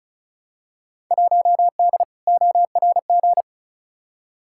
Morse code, heard as a keyed tone, about 36 wpm, 700 Hz. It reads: 1D ORG
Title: Field Day Exchanges - 35wpm